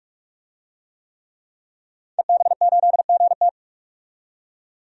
Morse code, heard as a keyed tone, about 43 wpm, 700 Hz. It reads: EB8GT
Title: DXCC-100 Call Signs - 5 Characters - 45wpm